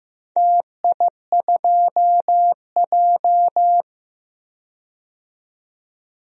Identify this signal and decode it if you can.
Morse code, heard as a keyed tone, about 15 wpm, 700 Hz. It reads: TI2J
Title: DXCC-100 Call Signs - 4 Characters - 15wpm